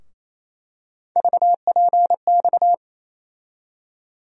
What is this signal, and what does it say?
Morse code, heard as a keyed tone, about 28 wpm, 700 Hz. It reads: VPX
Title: USA Call Sign Postfixes - 3 Letters 28wpm